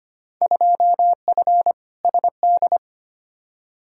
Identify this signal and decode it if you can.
Morse code, heard as a keyed tone, about 25 wpm, 700 Hz. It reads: 2F SD
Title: Field Day Exchanges Encore - 25wpm